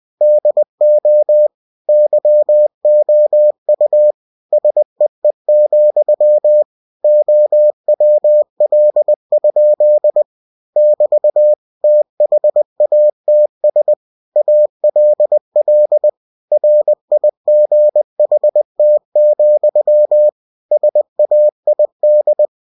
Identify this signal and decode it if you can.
Morse code, heard as a keyed tone, about 20 wpm, 600 Hz. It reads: DO YOU SEE, OWL? <BT> THATS ALL RIGHT, SAID